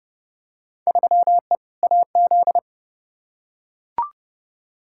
Morse code, heard as a keyed tone, about 30 wpm, 700 Hz. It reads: 3E AZ
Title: Field Day Exchanges Encore - 30wpm